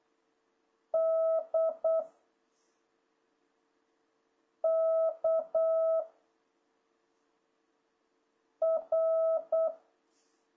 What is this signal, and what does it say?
Morse code, heard as a keyed tone, about 8 wpm, 660 Hz. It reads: D K R
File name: jh8cbh / 2026cw